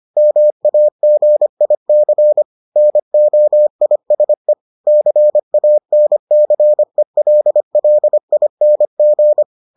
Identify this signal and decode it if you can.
Morse code, heard as a keyed tone, about 25 wpm, 600 Hz. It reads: MAGIC NOISE CANCELLING